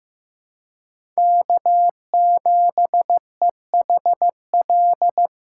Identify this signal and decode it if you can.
Morse code, heard as a keyed tone, about 15 wpm, 700 Hz. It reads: K7EHL
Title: USA Call Signs - 5 Characters - 4x Character Spacing - 15wpm